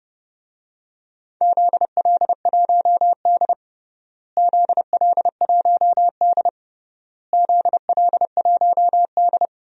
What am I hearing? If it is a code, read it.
Morse code, heard as a keyed tone, about 32 wpm, 700 Hz. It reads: ZL1D ZL1D ZL1D
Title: DXCC-100 Call Signs - 4 Characters - Repeated 3x - 30wpm